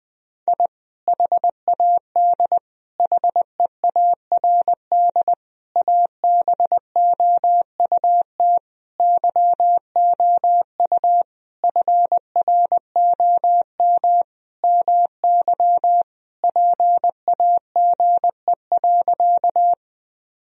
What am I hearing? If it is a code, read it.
Morse code, heard as a keyed tone, about 20 wpm, 700 Hz. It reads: I HAD HEARD ABOUT YOU FROM MY PAGE.
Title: Sentences from Top 1000 Words - Review Part 1 - 20wpm